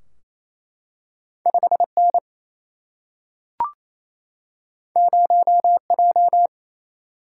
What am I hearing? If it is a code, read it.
Morse code, heard as a keyed tone, about 28 wpm, 700 Hz. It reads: 5N 0J
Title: Single-Letter Single-Number 28wpm